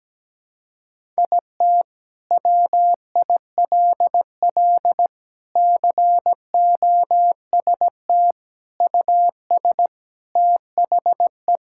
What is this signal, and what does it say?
Morse code, heard as a keyed tone, about 17 wpm, 700 Hz. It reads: IT WILL COST US THE
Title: Sentences from Top 1000 Words - Review Part 5 - 17wpm